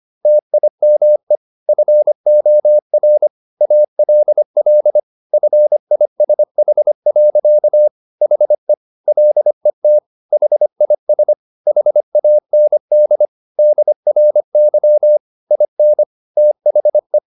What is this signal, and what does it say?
Morse code, heard as a keyed tone, about 25 wpm, 600 Hz. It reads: TIME FOR ALL FISH. HE LET HIS HAND DRY IN THE